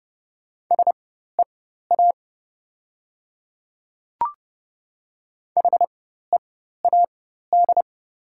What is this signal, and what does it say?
Morse code, heard as a keyed tone, about 30 wpm, spaced as for 14 wpm, 700 Hz. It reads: SEA HEAD
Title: 4x Character Spacing - Top 500 Words - 30wpm